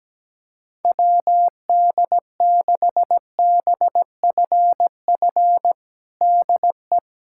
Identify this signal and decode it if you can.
Morse code, heard as a keyed tone, about 17 wpm, 700 Hz. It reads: WD6BFF DE
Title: USA Callsign DE Callsign - 6 Characters - 17wpm